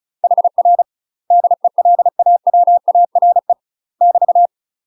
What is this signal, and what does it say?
Morse code, heard as a keyed tone, about 34 wpm, 700 Hz. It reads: HR DELAWARE <BT>
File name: UK MASTER QSOs TRAINING/35WPM